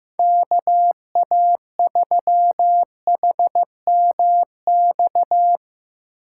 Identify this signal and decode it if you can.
Morse code, heard as a keyed tone, about 16 wpm, 700 Hz. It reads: KA3HMX
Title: USA Call Signs - 6 Characters - 3x Character Spacing - 15wpm